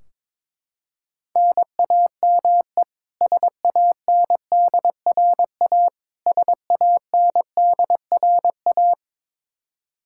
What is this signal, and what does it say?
Morse code, heard as a keyed tone, about 21 wpm, 700 Hz. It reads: NAME SANDRA SANDRA
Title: Practice QSO - Exchange 1 - Beginning - 22wpm